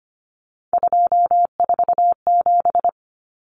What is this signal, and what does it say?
Morse code, heard as a keyed tone, about 25 wpm, 700 Hz. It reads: 247
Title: Numbers - 3-Digit 25wpm